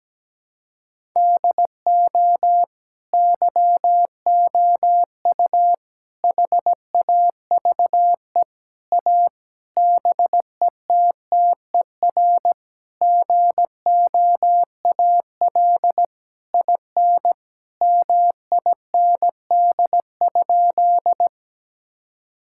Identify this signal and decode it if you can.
Morse code, heard as a keyed tone, about 17 wpm, 700 Hz. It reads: DO YOU HAVE A BETTER GOAL IN MIND?
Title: Sentences from Top 1000 Words - Review Part 5 - 17wpm